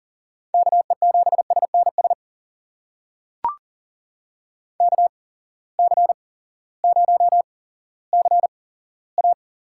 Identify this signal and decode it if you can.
Morse code, heard as a keyed tone, about 39 wpm, spaced as for 18 wpm, 700 Hz. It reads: KE7SNS KC0CA
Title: USA Call Signs - 6 Characters - 8x Character Spacing - 40wpm